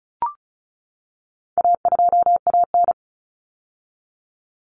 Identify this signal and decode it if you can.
Morse code, heard as a keyed tone, about 34 wpm, 700 Hz. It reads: A2AN
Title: Common Cut Numbers - 4-Digit 35wpm